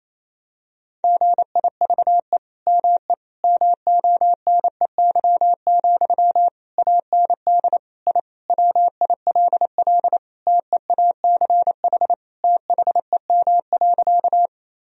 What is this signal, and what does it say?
Morse code, heard as a keyed tone, about 28 wpm, 700 Hz. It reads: GIVE ME MONEY, AND I WILL TEACH THEM.
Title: Sentences from Top 1000 Words - Review Part 2 - 28wpm